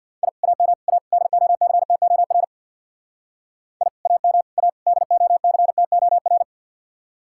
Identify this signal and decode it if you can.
Morse code, heard as a keyed tone, about 66 wpm, 700 Hz. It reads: I AM A DOCTOR I AM A DOCTOR